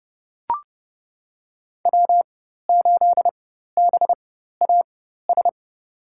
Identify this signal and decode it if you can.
Morse code, heard as a keyed tone, about 29 wpm, spaced as for 14 wpm, 700 Hz. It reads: W8BAS
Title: USA Call Signs - 5 Characters - 4x Character Spacing - 30wpm